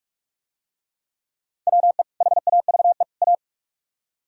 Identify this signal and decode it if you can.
Morse code, heard as a keyed tone, about 44 wpm, 690 Hz. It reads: WE HAVE A